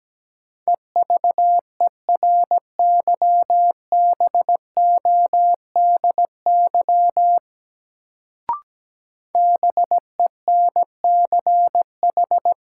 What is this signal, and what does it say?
Morse code, heard as a keyed tone, about 17 wpm, 700 Hz. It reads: EVERYBODY BENCH
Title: Sentences from Top 1400 Words - New Word Drill - 17wpm